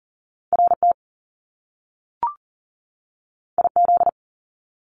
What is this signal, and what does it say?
Morse code, heard as a keyed tone, about 42 wpm, 700 Hz. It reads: RT IZ
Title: Top Two Letter Combinations in Words 40wpm